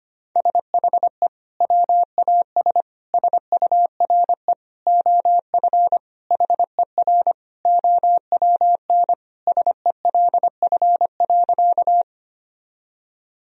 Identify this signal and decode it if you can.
Morse code, heard as a keyed tone, about 25 wpm, 700 Hz. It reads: SHE WAS SURE OF HER OWN SELF.
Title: Sentences from Top 1000 Words - Review Part 1 - 25wpm